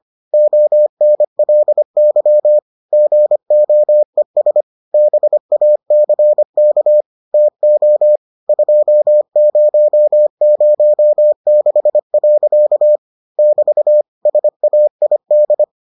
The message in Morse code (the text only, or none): ONLY GOES BACK TO 2006. <BT> SAID